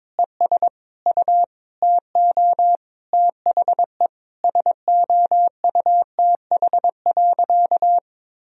Morse code, heard as a keyed tone, about 21 wpm, 700 Hz. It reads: ES U TO THE SOUTH.
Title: Sentences from Top 400 Words and Abbreviations 22wpm